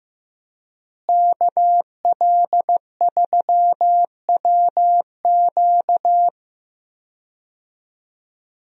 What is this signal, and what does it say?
Morse code, heard as a keyed tone, about 15 wpm, 700 Hz. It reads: KL3WQ
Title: USA Call Signs - 5 Characters - Repeated 3x - 15wpm